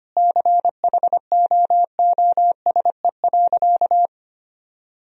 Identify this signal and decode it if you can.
Morse code, heard as a keyed tone, about 25 wpm, 700 Hz. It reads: CHOOSE.
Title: Sentences from Top 1000 Words - Review Part 4 - 25wpm